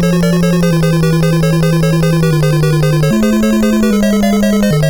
There is no code.